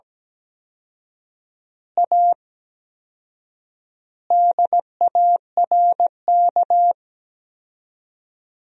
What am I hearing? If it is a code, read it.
Morse code, heard as a keyed tone, about 17 wpm, 700 Hz. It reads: A DARK
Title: 4x Word-Spacing - Sentences from Top 500 Words - 17wpm